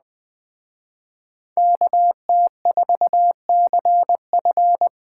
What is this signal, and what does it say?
Morse code, heard as a keyed tone, about 20 wpm, 700 Hz. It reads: KT4CF